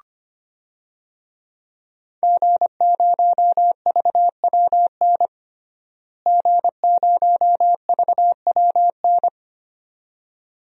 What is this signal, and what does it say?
Morse code, heard as a keyed tone, about 25 wpm, 700 Hz. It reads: G0VWN G0VWN
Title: DXCC-100 Call Signs - 5 Characters - Repeated 2x - 25wpm